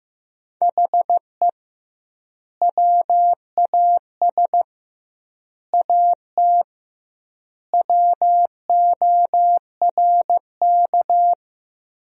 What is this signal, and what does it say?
Morse code, heard as a keyed tone, about 15 wpm, 700 Hz. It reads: HE WAS AT WORK
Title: 2x Word-Spacing - Sentences from Top 200 Words - 15wpm